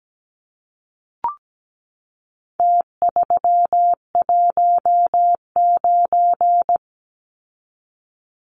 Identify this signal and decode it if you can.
Morse code, heard as a keyed tone, about 17 wpm, 700 Hz. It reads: T319
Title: Numbers - Contest Style - 4-Digit 17wpm